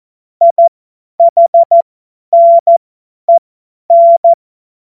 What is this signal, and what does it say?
Morse code, heard as a keyed tone, about 14 wpm, spaced as for 10 wpm, 680 Hz. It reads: IHNEN
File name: Morsekurs / Witze / Witz 25. Juli 2023 50 F70